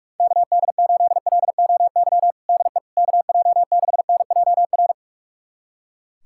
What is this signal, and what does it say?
Morse code, heard as a keyed tone, about 46 wpm, 700 Hz. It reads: KD8LOY DE KJ6NJR
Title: USA Callsign DE Callsign - 6 Characters - 45wpm